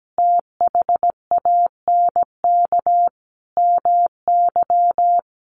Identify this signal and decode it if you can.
Morse code, heard as a keyed tone, about 17 wpm, 700 Hz. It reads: THANK MY
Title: Sentences from Top 1100 Words 17wpm